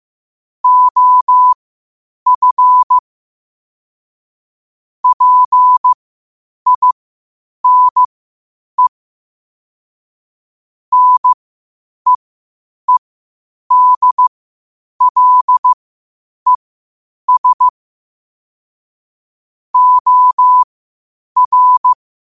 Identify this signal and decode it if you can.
Morse code, heard as a keyed tone, about 15 wpm, spaced as for 8 wpm, 1000 Hz. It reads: OF PINE NEEDLES OR
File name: qotdfiles / 8wpm